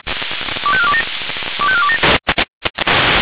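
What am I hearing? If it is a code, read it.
Five-tone selective call (ZVEI), heard as quick runs of five tones, tones of about 70 ms. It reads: 25427 25427
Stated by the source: Five-tone ZVEI selective call received on an SDR.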